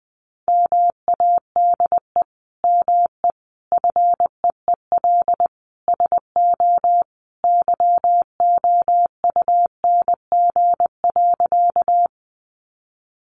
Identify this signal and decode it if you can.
Morse code, heard as a keyed tone, about 20 wpm, 700 Hz. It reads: MADE ME FEEL SO YOUNG.
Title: Sentences from Top 400 Words 20wpm